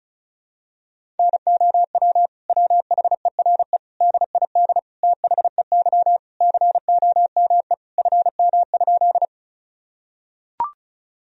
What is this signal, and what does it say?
Morse code, heard as a keyed tone, about 34 wpm, 700 Hz. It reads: NOW WHERE DID THEY COME FM?
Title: Sentences from Top 200 Words and Abbreviations 35wpm